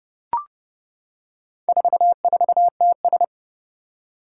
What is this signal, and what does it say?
Morse code, heard as a keyed tone, about 30 wpm, 700 Hz. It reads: 44TS
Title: Four Challenging Letter-Number Combinations 30wpm